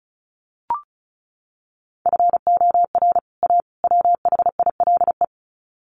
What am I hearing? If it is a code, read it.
Morse code, heard as a keyed tone, about 34 wpm, 700 Hz. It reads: FOR A WHILE
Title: Sets of 3 Words 35wpm